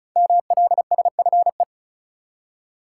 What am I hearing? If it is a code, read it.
Morse code, heard as a keyed tone, about 36 wpm, 700 Hz. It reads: MLSFE
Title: Code Groups - 5 Letters - 35wpm